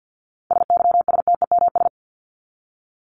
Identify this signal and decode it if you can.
Morse code, heard as a keyed tone, about 50 wpm, 700 Hz. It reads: SYSTEMS
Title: ES:S Suffix - Top 100 Words with Base Words 50wpm